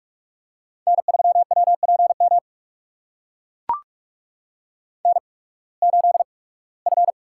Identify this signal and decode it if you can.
Morse code, heard as a keyed tone, about 44 wpm, spaced as for 24 wpm, 700 Hz. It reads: N3WPM N8F
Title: USA Call Signs - 5 Characters - 8x Character Spacing - 45wpm